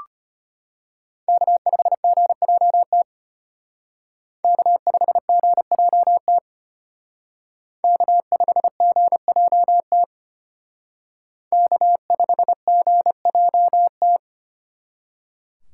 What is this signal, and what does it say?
Morse code, heard as a keyed tone, about 32 wpm, 700 Hz. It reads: K5GJT K5GJT K5GJT K5GJT